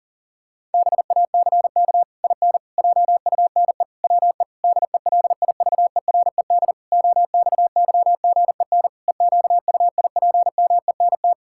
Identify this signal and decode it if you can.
Morse code, heard as a keyed tone, about 40 wpm, 700 Hz. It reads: BACK IN JUNE WE DELIVERED OXYGEN EQUIPMENT
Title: Pangram Challenge v2 - 40wpm